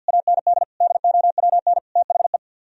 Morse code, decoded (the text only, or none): AND DOWN THE